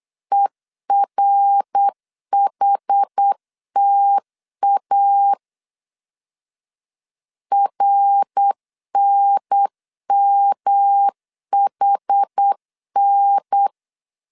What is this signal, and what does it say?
Morse code, heard as a keyed tone, about 8 wpm, 800 Hz. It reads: ERHTA RNMHN
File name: CW / DET / 1 3 1500C S B 8wpm 4141